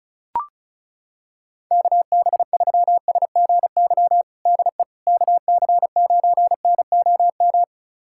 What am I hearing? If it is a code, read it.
Morse code, heard as a keyed tone, about 36 wpm, 700 Hz. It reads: KB3SGY DE KC9NOM
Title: USA Callsign DE Callsign - 6 Characters - 35wpm